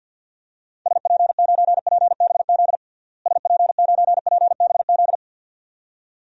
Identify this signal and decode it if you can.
Morse code, heard as a keyed tone, about 50 wpm, 700 Hz. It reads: SP9PBZ SP9PBZ
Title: DXCC-100 Call Signs - 6 Characters - Repeated 2x - 50wpm